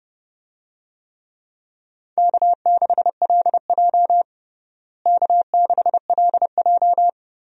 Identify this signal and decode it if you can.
Morse code, heard as a keyed tone, about 30 wpm, 700 Hz. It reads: K6LJ K6LJ
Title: USA Call Signs - 4 Characters - Repeated 3x - 30wpm